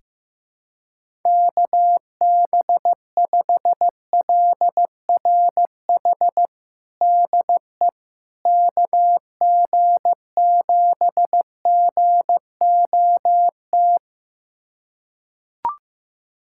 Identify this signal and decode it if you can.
Morse code, heard as a keyed tone, about 15 wpm, 700 Hz. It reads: KB5LRH DE KG7GOT
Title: USA Callsign DE Callsign 15wpm